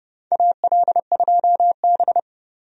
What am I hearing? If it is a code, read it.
Morse code, heard as a keyed tone, about 31 wpm, 700 Hz. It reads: AL2B
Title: USA Call Signs - 4 Characters - 30wpm